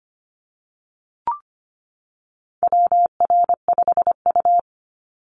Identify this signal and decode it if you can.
Morse code, heard as a keyed tone, about 25 wpm, 700 Hz. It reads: WR5U
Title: Call Signs - 25wpm